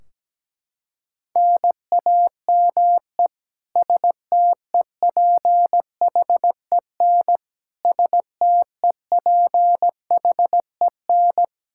Morse code, heard as a keyed tone, about 17 wpm, 700 Hz. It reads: NAME STEPHEN STEPHEN
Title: Practice QSO - Exchange 1 - Beginning - 17wpm